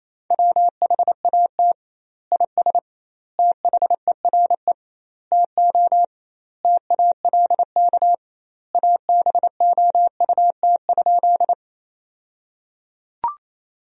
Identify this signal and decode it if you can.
Morse code, heard as a keyed tone, about 27 wpm, 700 Hz. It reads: WHAT IS THERE TO TALK ABOUT?
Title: 2x Word-Spacing - Sentences from Top 400 Words - 28wpm